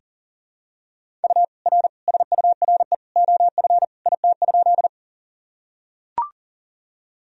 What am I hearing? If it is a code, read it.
Morse code, heard as a keyed tone, about 40 wpm, 700 Hz. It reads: U R SURE OF IT?
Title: Sentences from Top 300 Words and Abbreviations 40wpm